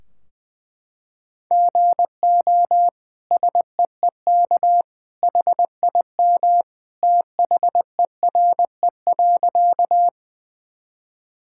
Morse code, heard as a keyed tone, about 20 wpm, 700 Hz. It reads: GO SEEK HIM THERE.
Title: Sentences from Top 1600 Words 20wpm